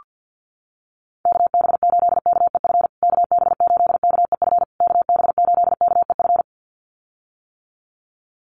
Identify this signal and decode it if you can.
Morse code, heard as a keyed tone, about 48 wpm, 700 Hz. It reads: KB8KEF KB8KEF KB8KEF
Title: USA Call Signs Repeated 3x 50wpm